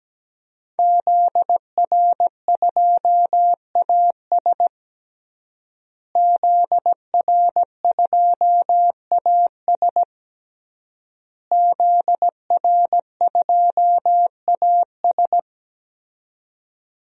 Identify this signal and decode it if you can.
Morse code, heard as a keyed tone, about 17 wpm, 700 Hz. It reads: ZR2AS ZR2AS ZR2AS
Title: DXCC-100 Call Signs - 5 Characters - Repeated 3x - 17wpm